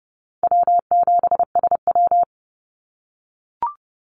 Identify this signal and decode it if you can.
Morse code, heard as a keyed tone, about 32 wpm, 700 Hz. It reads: W7SW
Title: Easy Call Signs - 30wpm